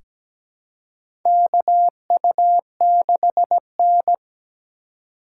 Morse code, heard as a keyed tone, about 17 wpm, 700 Hz. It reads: KU6N
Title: USA Call Signs - 4 Characters - 4x Character Spacing - 17wpm